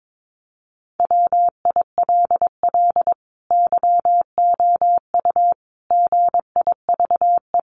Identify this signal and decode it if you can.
Morse code, heard as a keyed tone, about 22 wpm, 700 Hz. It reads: WILL YOU GIVE
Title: Sentences from Top 200 Words 22wpm